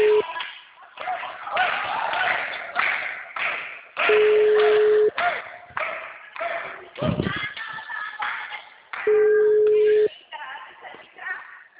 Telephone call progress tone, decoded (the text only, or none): ringback tone